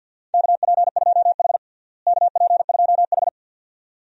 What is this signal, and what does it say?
Morse code, heard as a keyed tone, about 47 wpm, 700 Hz. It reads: KP2H KP2H
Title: DXCC-100 Call Signs - 4 Characters - Repeated 2x - 50wpm